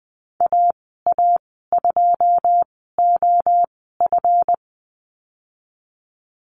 Morse code, heard as a keyed tone, about 20 wpm, spaced as for 15 wpm, 700 Hz. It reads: AA2OF
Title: 186 Lesson 30g - Letter b - Call Signs 20wpm